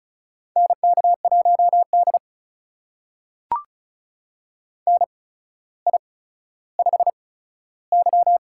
Morse code, heard as a keyed tone, about 36 wpm, spaced as for 15 wpm, 700 Hz. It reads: NK1D NI5Y
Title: USA Call Signs - 4 Characters - 8x Character Spacing - 35wpm